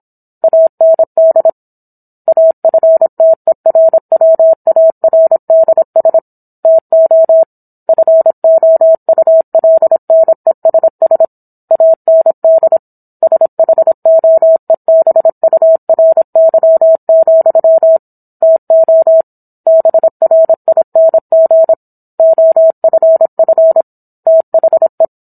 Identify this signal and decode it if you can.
Morse code, heard as a keyed tone, about 27 wpm, 650 Hz. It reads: AND AFTERWARDS TO FOULNESS AND SHOEBURY, TO BRING OFF THE